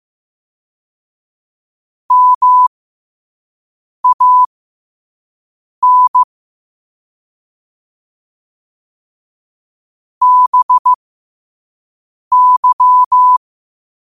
Morse code, heard as a keyed tone, about 15 wpm, spaced as for 5 wpm, 1000 Hz. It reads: MAN BY
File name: qotdfiles / 5wpm